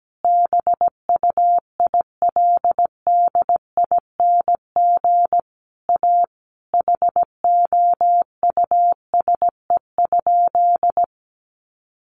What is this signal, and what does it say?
Morse code, heard as a keyed tone, about 17 wpm, 700 Hz. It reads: BUILDING A HOUSE?
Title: Sentences from Top 700 Words 17wpm